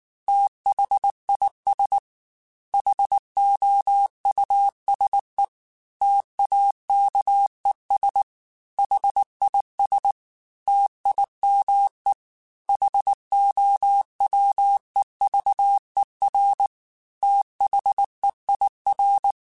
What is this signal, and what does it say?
Morse code, heard as a keyed tone, about 19 wpm, 800 Hz. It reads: THIS HOUSE TAKES HIS TIME HOWEVER THEIR
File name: T then H Runs / T then H Runs.mp3 / T then H Runs @18